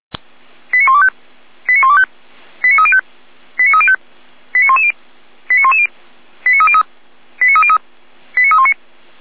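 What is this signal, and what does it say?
Five-tone selective call (ZVEI), heard as quick runs of five tones, tones of about 70 ms. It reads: 89125 89125 89305 89305 89110 89110 89303 89303 89319
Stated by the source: ZVEI five-tone selective call received on SDR.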